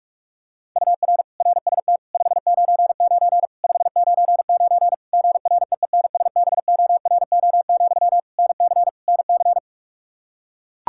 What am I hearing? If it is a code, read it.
Morse code, heard as a keyed tone, about 41 wpm, 700 Hz. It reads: UR RST 599 599 GREENSBORO, NC NC
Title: Practice QSO - Exchange 1 - Beginning - 45wpm